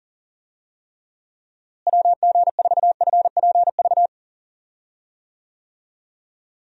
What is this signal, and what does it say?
Morse code, heard as a keyed tone, about 40 wpm, 700 Hz. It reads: WG4FPV
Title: USA Call Signs - 6 Characters - Repeated 2x - 40wpm